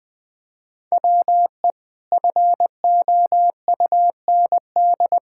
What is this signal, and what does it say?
Morse code, heard as a keyed tone, about 20 wpm, 700 Hz. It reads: WE FOUND
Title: Sentences from Top 1000 Words - Review Part 5 - 20wpm